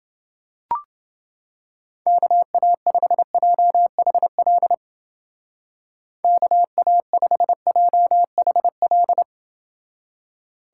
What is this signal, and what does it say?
Morse code, heard as a keyed tone, about 30 wpm, 700 Hz. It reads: KA5JHL KA5JHL